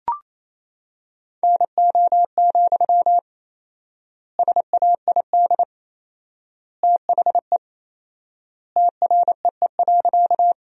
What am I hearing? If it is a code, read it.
Morse code, heard as a keyed tone, about 28 wpm, 700 Hz. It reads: NO, SAID THE TREE.
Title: 4x Word-Spacing - Sentences from Top 300 Words - 28wpm